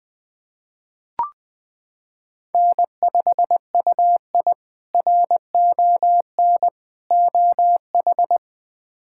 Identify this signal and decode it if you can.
Morse code, heard as a keyed tone, about 20 wpm, 700 Hz. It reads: N5UI RON OH
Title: K1USN SST QSO Practice - 20wpm